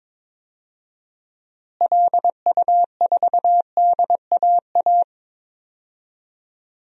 Morse code, heard as a keyed tone, about 21 wpm, 700 Hz. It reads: LU4DAA
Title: DXCC-100 Call Signs - 6 Characters - Repeated 3x - 22wpm